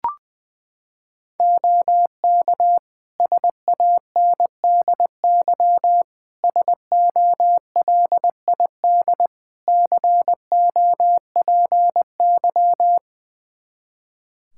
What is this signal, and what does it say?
Morse code, heard as a keyed tone, about 21 wpm, 700 Hz. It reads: OK SANDY SOLID COPY
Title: Practice QSO - Exchange 3 - More Info - 20wpm